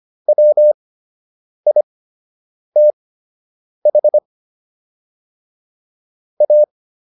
Morse code, heard as a keyed tone, about 25 wpm, spaced as for 8 wpm, 600 Hz. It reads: WITH A